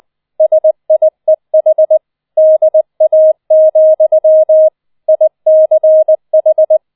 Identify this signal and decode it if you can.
Morse code, heard as a keyed tone, about 20 wpm, 630 Hz. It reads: SIEH DA, ICH